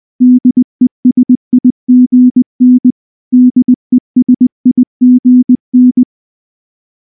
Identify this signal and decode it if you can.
Morse code, heard as a keyed tone, about 20 wpm, 250 Hz. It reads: DESIGN DESIGN
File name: morse / morse-training-20WPM-6